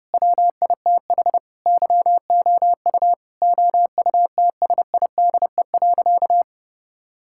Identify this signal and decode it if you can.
Morse code, heard as a keyed tone, about 30 wpm, 700 Hz. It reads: WITH YOU OUTSIDE.
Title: Sentences from Top 1000 Words - Review Part 6 - 30wpm